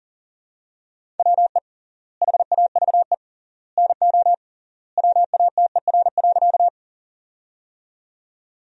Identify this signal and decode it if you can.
Morse code, heard as a keyed tone, about 40 wpm, 700 Hz. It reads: WE HAVE NO WATER.
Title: 3x Word-Spacing - Sentences from Top 100 Words - 40wpm